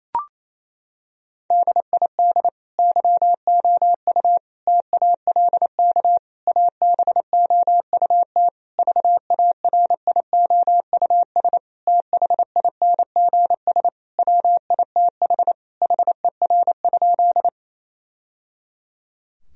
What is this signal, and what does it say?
Morse code, heard as a keyed tone, about 28 wpm, 700 Hz. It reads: DID YOU TALK ABOUT VARIOUS THINGS WITH HER?
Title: Sentences from Top 1000 Words - Review Part 4 - 28wpm